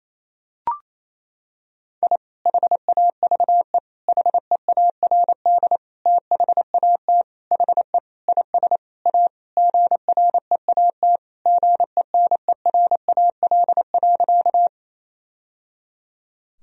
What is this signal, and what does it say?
Morse code, heard as a keyed tone, about 28 wpm, 700 Hz. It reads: I HAVE HEARD THAT HE IS A GREAT GENERAL.
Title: Sentences from Top 1000 Words - Review Part 3 - 28wpm